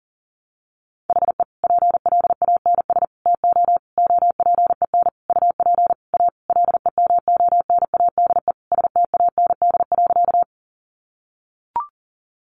Morse code, heard as a keyed tone, about 37 wpm, 700 Hz. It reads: HE PLANS TO OPEN UP A LEMONADE STAND.
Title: Additional Easy Phrases 40wpm